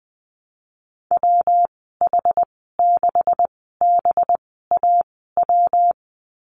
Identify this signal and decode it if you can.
Morse code, heard as a keyed tone, about 19 wpm, spaced as for 14 wpm, 700 Hz. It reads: WH6BAW
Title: 193 Lesson 31g - Letter v - Call Signs 20wpm